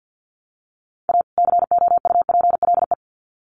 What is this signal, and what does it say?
Morse code, heard as a keyed tone, about 50 wpm, 700 Hz. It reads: A COUPLE
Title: Sets of 2 Words 50wpm